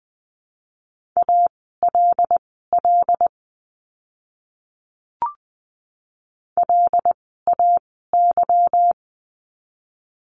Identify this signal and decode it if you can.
Morse code, heard as a keyed tone, about 20 wpm, spaced as for 15 wpm, 700 Hz. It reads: ALL LAY
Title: 146 Lesson 24f - Letter y - Words 20wpm